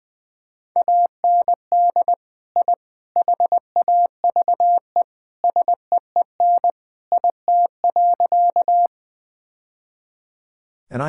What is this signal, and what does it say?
Morse code, heard as a keyed tone, about 20 wpm, 700 Hz. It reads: AND I HAVE SEEN IT.
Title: Sentences with 3-Word Phrases - 20wpm